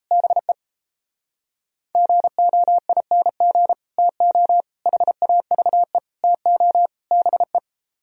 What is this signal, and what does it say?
Morse code, heard as a keyed tone, about 36 wpm, 700 Hz. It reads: BE GOING TO HAVE TO BE